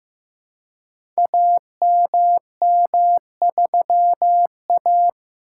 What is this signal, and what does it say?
Morse code, heard as a keyed tone, about 15 wpm, 700 Hz. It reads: AMM3A
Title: Code Groups - 5 Letter:Numbers - 15wpm